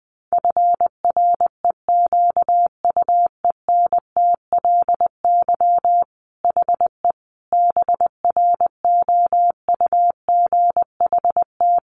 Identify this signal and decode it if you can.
Morse code, heard as a keyed tone, about 20 wpm, 700 Hz. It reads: FREQUENTLY HE BROUGHT
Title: Sentences from Top 1300 Words 20wpm